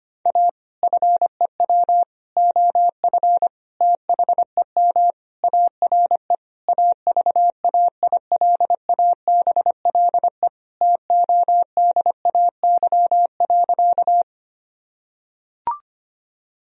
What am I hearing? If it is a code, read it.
Morse code, heard as a keyed tone, about 25 wpm, 700 Hz. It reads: A FEW OF THEM ARE AVAILABLE TODAY.
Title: Sentences from Top 1000 Words - Review Part 3 - 25wpm